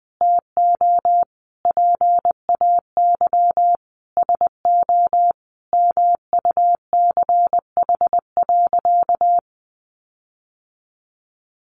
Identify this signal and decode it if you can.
Morse code, heard as a keyed tone, about 20 wpm, 700 Hz. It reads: TO PAY SO MUCH.
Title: Sentences from Top 1400 Words 20wpm